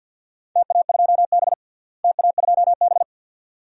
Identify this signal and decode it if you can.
Morse code, heard as a keyed tone, about 52 wpm, 700 Hz. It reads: TA2B TA2B
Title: DXCC-100 Call Signs - 4 Characters - Repeated 2x - 50wpm